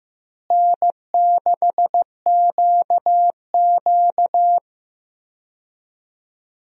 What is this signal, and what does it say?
Morse code, heard as a keyed tone, about 15 wpm, 700 Hz. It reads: N6QQ
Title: USA Call Signs - 4 Characters - Repeated 2x - 15wpm